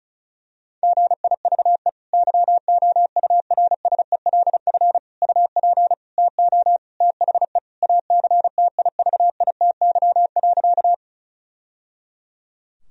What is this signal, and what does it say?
Morse code, heard as a keyed tone, about 36 wpm, 700 Hz. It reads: GIVE YOURSELF UP TO THE ACTIVITY.
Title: Sentences from Top 1000 Words - Review Part 4 - 35wpm